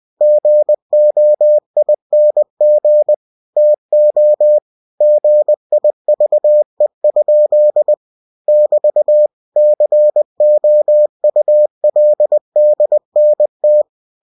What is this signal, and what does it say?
Morse code, heard as a keyed tone, about 20 wpm, 600 Hz. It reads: GOING TO GIVE? <BT> COULDNT
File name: cw-books / winnie-the-pooh / winnie-20wpm-page-0049